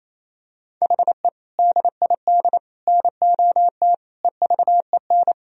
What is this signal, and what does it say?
Morse code, heard as a keyed tone, about 28 wpm, 700 Hz. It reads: HE DID NOT EVEN
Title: Sentences from Top 1000 Words - Review Part 1 - 28wpm